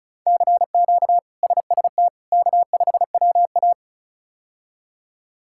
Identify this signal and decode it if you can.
Morse code, heard as a keyed tone, about 36 wpm, 700 Hz. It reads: CQ SST K5WA
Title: K1USN SST QSO Practice - 35wpm